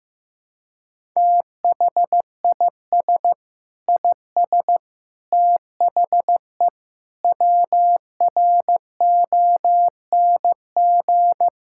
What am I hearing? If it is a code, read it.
Morse code, heard as a keyed tone, about 15 wpm, 700 Hz. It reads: THIS IS THE WRONG
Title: Sentences from Top 1000 Words - Review Part 6 - 15wpm